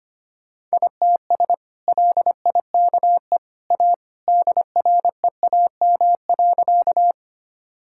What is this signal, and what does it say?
Morse code, heard as a keyed tone, about 25 wpm, 690 Hz. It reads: ITS LIKE A DREAM.
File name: practice / 25wpm / Sentences with 2-Word Phrases - 25wpm